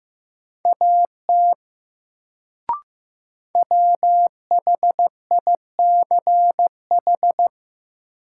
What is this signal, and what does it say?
Morse code, heard as a keyed tone, about 15 wpm, 700 Hz. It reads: AT WHICH
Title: Sentences from Top 100 Words - New Word Drill - 15wpm